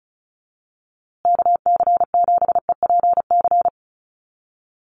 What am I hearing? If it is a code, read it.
Morse code, heard as a keyed tone, about 34 wpm, 700 Hz. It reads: KC7EPC
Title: USA Call Signs Repeated 3x 35wpm